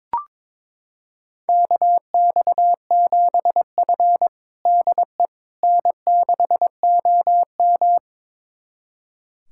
Morse code, heard as a keyed tone, about 22 wpm, 700 Hz. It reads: KX7F DE N6OM
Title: USA Callsign DE Callsign - 4 Characters - 22wpm